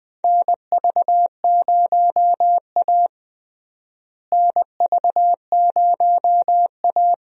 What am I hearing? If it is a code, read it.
Morse code, heard as a keyed tone, about 21 wpm, 700 Hz. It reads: NV0A NV0A
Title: USA Call Signs - 4 Characters - Repeated 2x - 20wpm